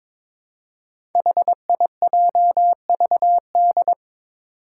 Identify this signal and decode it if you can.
Morse code, heard as a keyed tone, about 22 wpm, 700 Hz. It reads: HIJVD
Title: Five Challenging Letter-Number Combinations 22wpm